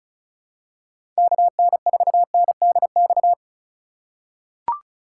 Morse code, heard as a keyed tone, about 34 wpm, 690 Hz. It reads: KN4NDX
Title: USA Call Signs - 3x Character Spacing - 35wpm